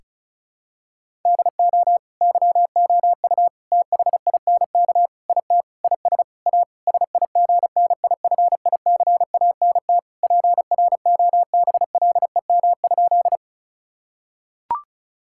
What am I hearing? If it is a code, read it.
Morse code, heard as a keyed tone, about 36 wpm, 700 Hz. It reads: DO YOU THINK IT IS A SIGNIFICANT PROBLEM?
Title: Sentences from Top 1000 Words - Review Part 6 - 35wpm